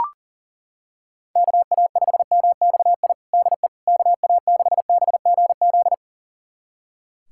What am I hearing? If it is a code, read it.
Morse code, heard as a keyed tone, about 40 wpm, 700 Hz. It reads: KA5MXI DE KA6BGZ
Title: USA Callsign DE Callsign - 6 Characters - 40wpm